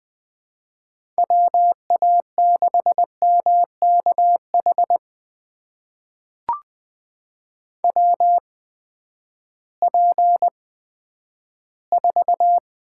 Morse code, heard as a keyed tone, about 20 wpm, spaced as for 11 wpm, 700 Hz. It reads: WA6MKH WP4
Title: USA Call Signs - 8x Character Spacing - 20wpm